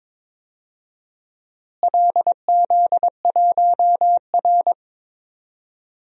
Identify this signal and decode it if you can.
Morse code, heard as a keyed tone, about 22 wpm, 700 Hz. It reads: LZ1R
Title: DXCC-100 Call Signs - 4 Characters - Repeated 2x - 22wpm